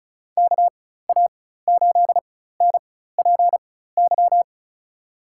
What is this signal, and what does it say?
Morse code, heard as a keyed tone, about 36 wpm, spaced as for 16 wpm, 700 Hz. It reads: KA8NPY
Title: USA Call Signs - 6 Characters - 4x Character Spacing - 35wpm